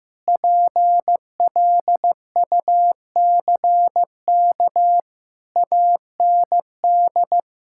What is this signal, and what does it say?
Morse code, heard as a keyed tone, about 16 wpm, 700 Hz. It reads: PLUCK AND
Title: Pangram Challenge v2 - 15wpm